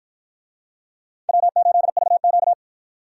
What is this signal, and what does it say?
Morse code, heard as a keyed tone, about 53 wpm, 700 Hz. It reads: W8VQ